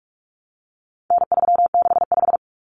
Morse code, heard as a keyed tone, about 44 wpm, 700 Hz. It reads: N365
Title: Common Cut Numbers - 4-Digit 45wpm